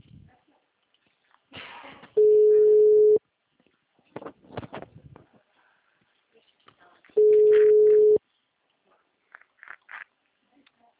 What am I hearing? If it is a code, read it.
Ringback tone (Europe, 425 Hz).
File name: out- -2012- - - .32153